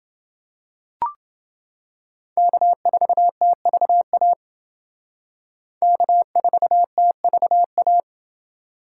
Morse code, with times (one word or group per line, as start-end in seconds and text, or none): 2.4-4.3 K4TVA
5.8-8.0 K4TVA